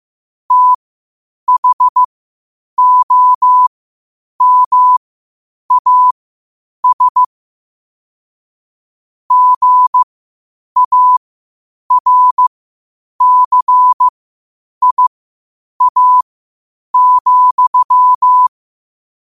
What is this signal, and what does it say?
Morse code, heard as a keyed tone, about 15 wpm, spaced as for 8 wpm, 1000 Hz. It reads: THOMAS GARCIA,
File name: qotdfiles / 8wpm